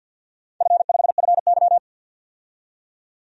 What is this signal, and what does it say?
Morse code, heard as a keyed tone, about 50 wpm, 700 Hz. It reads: F5FY
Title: DXCC-100 Call Signs - 4 Characters - 50wpm